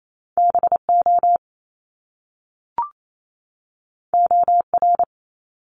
Morse code, heard as a keyed tone, about 28 wpm, 700 Hz. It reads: BO OR
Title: Top Two Letter Combinations in Words 28wpm